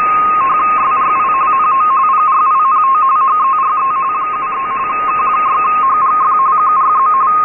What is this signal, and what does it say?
RTTY, about 45.45 baud. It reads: SRI NO PRINT HERE 73 K6HGF QRZ
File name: rtty-D2000Hz